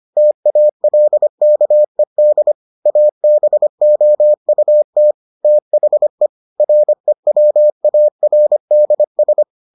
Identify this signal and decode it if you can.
Morse code, heard as a keyed tone, about 25 wpm, 600 Hz. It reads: TALKED ABOUT THE REWARDS